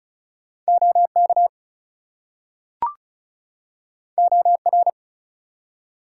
Morse code, heard as a keyed tone, about 36 wpm, 700 Hz. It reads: OK OR
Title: Two Letter Words - 35wpm